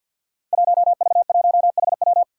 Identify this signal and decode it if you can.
Morse code, heard as a keyed tone, about 52 wpm, 700 Hz. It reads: 1V1HW
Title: Five Challenging Letter-Number Combinations 50wpm